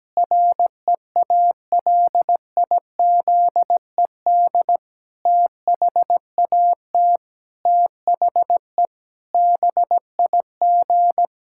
Morse code, heard as a keyed tone, about 17 wpm, 700 Hz. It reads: REALIZED THAT THE BIG
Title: Pangram Challenge v2 - 17wpm